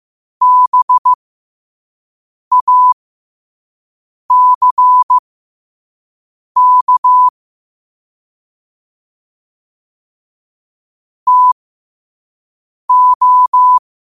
Morse code, heard as a keyed tone, about 15 wpm, spaced as for 5 wpm, 1000 Hz. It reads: BACK TO
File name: qotdfiles / 5wpm